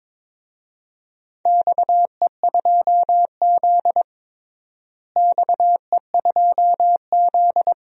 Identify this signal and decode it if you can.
Morse code, heard as a keyed tone, about 22 wpm, 700 Hz. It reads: XE2Z XE2Z
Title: DXCC-100 Call Signs - 4 Characters - Repeated 2x - 22wpm